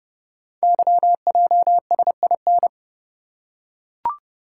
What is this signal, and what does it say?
Morse code, heard as a keyed tone, about 30 wpm, 700 Hz. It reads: YJSIN
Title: Code Groups - 5 Letters - 30wpm